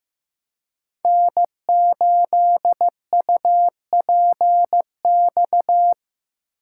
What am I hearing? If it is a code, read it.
Morse code, heard as a keyed tone, about 15 wpm, 700 Hz. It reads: N8UPX
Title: USA Call Signs - 6x Character Spacing - 15wpm